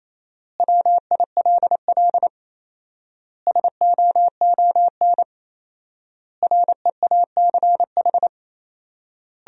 Morse code, heard as a keyed tone, about 28 wpm, 700 Hz. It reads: WILL SOON REACH
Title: 4x Word-Spacing - Sentences from Top 400 Words - 28wpm